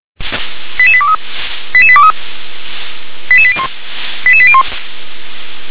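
Five-tone selective call (ZVEI), heard as quick runs of five tones, tones of about 70 ms. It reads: 88823 88823 88801 88801
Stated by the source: ZVEI five-tone selective call 88052, received by SDR.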